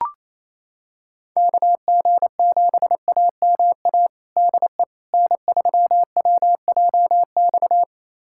Morse code, heard as a keyed tone, about 28 wpm, 700 Hz. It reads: KG7AMA DE N3WJX
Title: USA Callsign DE Callsign 28wpm